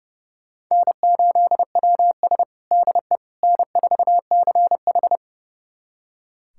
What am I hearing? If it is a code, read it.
Morse code, heard as a keyed tone, about 30 wpm, 700 Hz. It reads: N8WS DE N4CH
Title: USA Callsign DE Callsign - 4 Characters - 30wpm